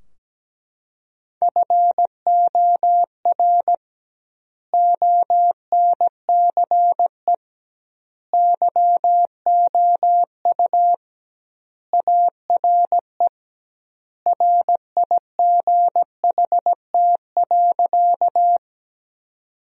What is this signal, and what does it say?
Morse code, heard as a keyed tone, about 17 wpm, 700 Hz. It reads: FOR ONCE YOU ARE RIGHT.
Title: 2x Word-Spacing - Sentences from Top 300 Words - 17wpm